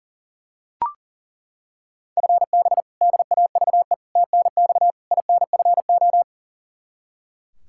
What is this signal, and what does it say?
Morse code, heard as a keyed tone, about 40 wpm, 700 Hz. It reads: FB DAVE TNX INFO
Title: K1USN SST QSO Practice 40wpm